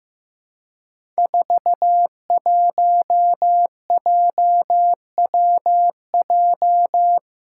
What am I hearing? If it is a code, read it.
Morse code, heard as a keyed tone, about 15 wpm, 700 Hz. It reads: 41JWJ
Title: Five Challenging Letter-Number Combinations 15wpm